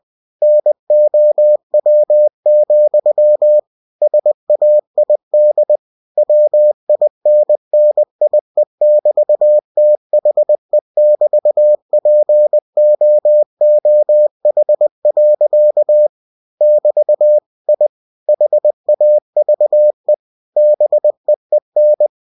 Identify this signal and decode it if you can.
Morse code, heard as a keyed tone, about 20 wpm, 600 Hz. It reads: NOW, SAID WINNIE<BT>THE<BT>POOH. <BT> I HAVE BEEN
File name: cw-books / winnie-the-pooh / winnie-20wpm-page-0025